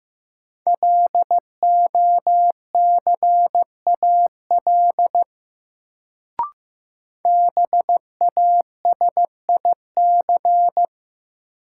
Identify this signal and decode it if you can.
Morse code, heard as a keyed tone, about 15 wpm, 700 Hz. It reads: LOCAL BASIC
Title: Sentences from Top 1000 Words - Review Part 3 - New Word Drill - 15wpm